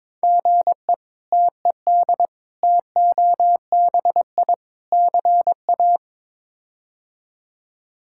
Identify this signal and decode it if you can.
Morse code, heard as a keyed tone, about 22 wpm, 700 Hz. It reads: GE TED TOBI CA
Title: K1USN SST QSO Practice - 22wpm